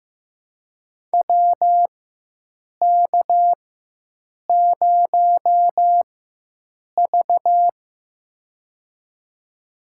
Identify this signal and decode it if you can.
Morse code, heard as a keyed tone, about 16 wpm, spaced as for 7 wpm, 700 Hz. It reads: WK0V
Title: USA Call Signs - 4 Characters - 4x Character Spacing - 15wpm